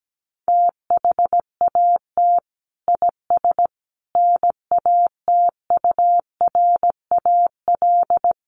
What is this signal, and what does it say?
Morse code, heard as a keyed tone, about 17 wpm, 700 Hz. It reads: THAT IS NATURAL
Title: Sentences from Top 700 Words 17wpm